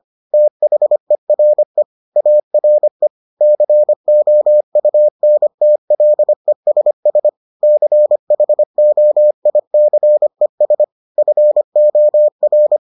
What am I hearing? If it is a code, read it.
Morse code, heard as a keyed tone, about 25 wpm, 600 Hz. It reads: THERE ARE COUNTLESS CHOICES FOR